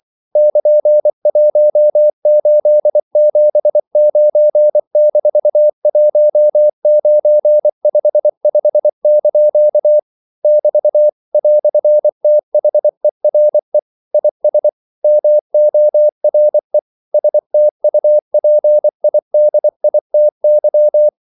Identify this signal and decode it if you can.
Morse code, heard as a keyed tone, about 24 wpm, 600 Hz. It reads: <KN>1879-1955) <BT> "THERE IS MORE STUPIDITY
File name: CW / Practice / EnglishQuotations0224